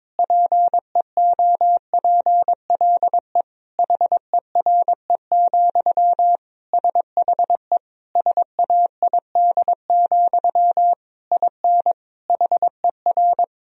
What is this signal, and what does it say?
Morse code, heard as a keyed tone, about 22 wpm, 700 Hz. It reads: PEOPLE HERE, SHE SAID, IN HER
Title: Sentences from Top 1000 Words - Review Part 1 - 22wpm